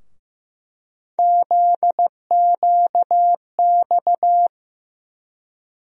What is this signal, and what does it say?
Morse code, heard as a keyed tone, about 16 wpm, 700 Hz. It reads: ZQX
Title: USA Call Sign Postfixes - 3 Letters 15wpm